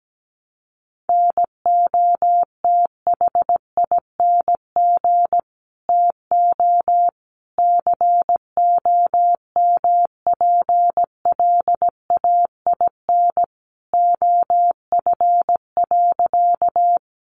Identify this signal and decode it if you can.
Morse code, heard as a keyed tone, about 17 wpm, 700 Hz. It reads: NOTHING TO COMPLAIN OF.
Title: Sentences from Top 1300 Words 17wpm